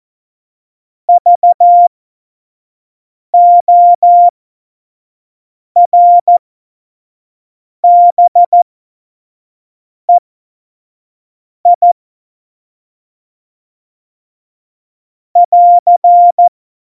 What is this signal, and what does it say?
Morse code, heard as a keyed tone, about 14 wpm, spaced as for 5 wpm, 700 Hz. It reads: VORBEI <AR>
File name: Morsekurs / Witze / Witz 17. November 2020, 25 F70